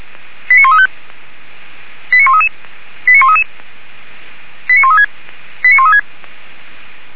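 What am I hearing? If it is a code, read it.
Five-tone selective call (ZVEI), heard as quick runs of five tones, tones of about 70 ms. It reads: 89135 89133 89133 89136 89136
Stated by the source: Received on an SDR, ZVEI five-tone call.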